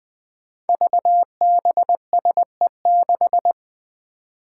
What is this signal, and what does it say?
Morse code, heard as a keyed tone, about 20 wpm, 700 Hz. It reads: VBSE6
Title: Five Challenging Letter-Number Combinations 20wpm